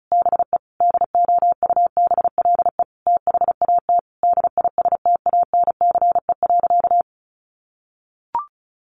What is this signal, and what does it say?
Morse code, heard as a keyed tone, about 34 wpm, 700 Hz. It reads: BE DOUBLE THAT DISTANCE.
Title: Sentences from Top 1400 Words 35wpm